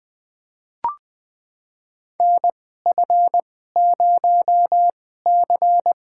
Morse code, heard as a keyed tone, about 20 wpm, spaced as for 14 wpm, 700 Hz. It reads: NF0C
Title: USA Call Signs - 4 Characters - 2x Character Spacing - 20wpm